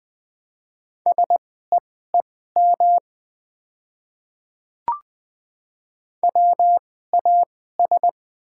Morse code, heard as a keyed tone, about 20 wpm, spaced as for 14 wpm, 700 Hz. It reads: SEEM WAS
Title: 2x Character Spacing - Top 300 Words - 20wpm